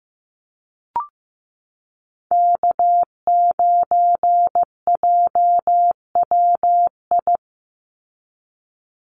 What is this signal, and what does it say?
Morse code, heard as a keyed tone, about 15 wpm, 700 Hz. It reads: K9JWI
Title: Call Signs Repeated 2x - 15wpm